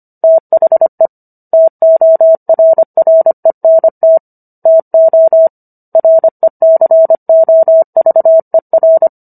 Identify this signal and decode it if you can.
Morse code, heard as a keyed tone, about 25 wpm, 650 Hz. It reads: THE TORRENT TO RECOVER
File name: Book One / 16-Chapter 16 - 25 WPM